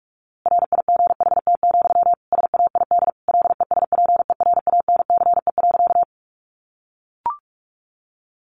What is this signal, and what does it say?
Morse code, heard as a keyed tone, about 44 wpm, 700 Hz. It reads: RIGHT, SAID LESPERANCE.
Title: A Sound Of Thunder 45wpm